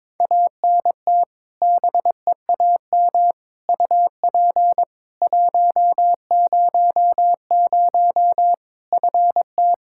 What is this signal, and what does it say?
Morse code, heard as a keyed tone, about 22 wpm, 700 Hz. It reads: ANT BEAM UP 100 FT
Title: K1USN SST QSO Practice 22wpm